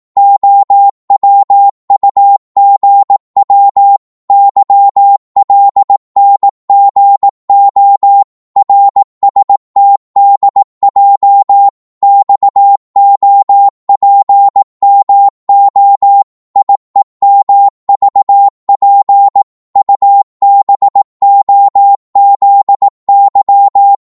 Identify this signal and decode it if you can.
Morse code, heard as a keyed tone, about 18 wpm, 800 Hz. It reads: OWUGW YLNGO RSTDJ XOPMO IEMVP UBOZY
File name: Morse / ltr-18wpm